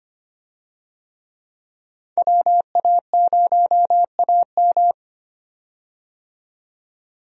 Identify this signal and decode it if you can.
Morse code, heard as a keyed tone, about 25 wpm, 700 Hz. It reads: WA0AM
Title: USA Call Signs - 5 Characters - 25wpm